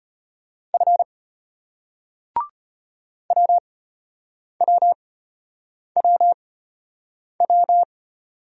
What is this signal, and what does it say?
Morse code, heard as a keyed tone, about 35 wpm, spaced as for 8 wpm, 700 Hz. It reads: F WWWW